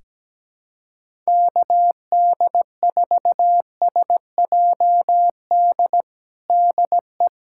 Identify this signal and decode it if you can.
Morse code, heard as a keyed tone, about 17 wpm, 700 Hz. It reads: KD4SJD DE
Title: USA Callsign DE Callsign 17wpm